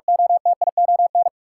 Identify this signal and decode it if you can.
Morse code, heard as a keyed tone, about 46 wpm, 700 Hz. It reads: OTION
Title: Sentences from Top 1600 Words - New Word Drill - 45wpm